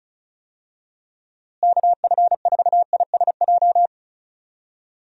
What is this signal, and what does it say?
Morse code, heard as a keyed tone, about 34 wpm, 700 Hz. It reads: KF4ISJ
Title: USA Call Signs - 6 Characters - Repeated 3x - 35wpm